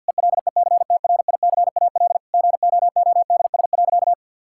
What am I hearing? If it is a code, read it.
Morse code, heard as a keyed tone, about 50 wpm, 700 Hz. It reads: ELECTRICAL GOODS.
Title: Sentences from Top 1000 Words - Review Part 6 - 50wpm